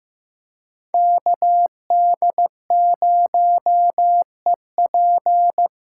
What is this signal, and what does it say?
Morse code, heard as a keyed tone, about 18 wpm, 700 Hz. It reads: KD0EP
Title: Speed Racing - USA 2x2 Call Signs 15wpm